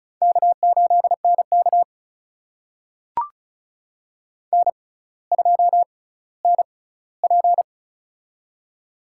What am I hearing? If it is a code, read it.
Morse code, heard as a keyed tone, about 36 wpm, spaced as for 18 wpm, 700 Hz. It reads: K8NK N2NP
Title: USA Call Signs - 4 Characters - 6x Character Spacing - 35wpm